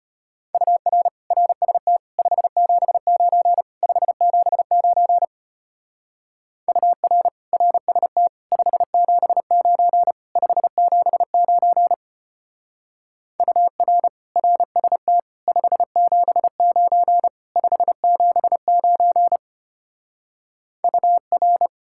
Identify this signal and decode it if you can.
Morse code, heard as a keyed tone, about 31 wpm, 700 Hz. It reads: UR RST 579 579 UR RST 579 579 UR RST 579 579 UR
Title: Speed-Racing Common QSO Phrases - 25wpm